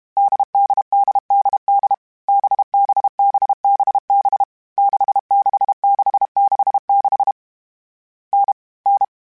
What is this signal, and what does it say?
Morse code, heard as a keyed tone, about 32 wpm, 800 Hz. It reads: DDDDD BBBBB 66666 N N
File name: Variable Speed / Variable Speed-mp3s / GZ GZ7 sp @30